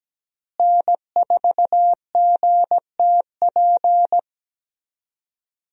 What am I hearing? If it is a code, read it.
Morse code, heard as a keyed tone, about 17 wpm, 700 Hz. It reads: N4GTP
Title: USA Call Signs - 5 Characters - Repeated 2x - 17wpm